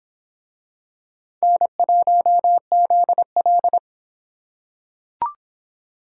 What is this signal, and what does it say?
Morse code, heard as a keyed tone, about 23 wpm, 700 Hz. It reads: N1ZL